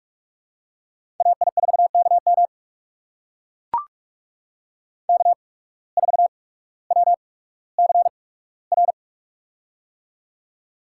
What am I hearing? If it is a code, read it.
Morse code, heard as a keyed tone, about 44 wpm, spaced as for 19 wpm, 700 Hz. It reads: AI4KM K4WCR
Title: USA Call Signs - 5 Characters - 8x Character Spacing - 45wpm